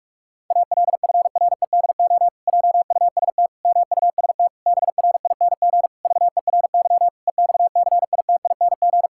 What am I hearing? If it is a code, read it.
Morse code, heard as a keyed tone, about 46 wpm, 700 Hz. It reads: ALFREDO JUST MUST BRING VERY EXCITING
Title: Pangram Challenge v2 - 45wpm